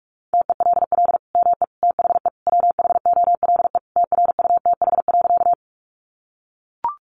Morse code, heard as a keyed tone, about 44 wpm, 700 Hz. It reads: TELL ME THE WHOLE TRUTH.
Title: Sentences from Top 800 Words 45wpm